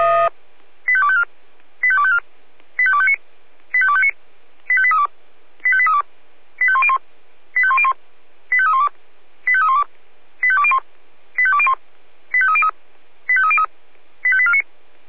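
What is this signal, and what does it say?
Five-tone selective call (ZVEI), heard as quick runs of five tones, tones of about 70 ms. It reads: 85254 85254 85259 85259 85832 85832 85101 85101 85312 85312 85201 85201 85303 85303 85849
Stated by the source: ZVEI five-tone call received on SDR, recorded by the Orlamünde fire brigade (Feuerwehr Orlamünde).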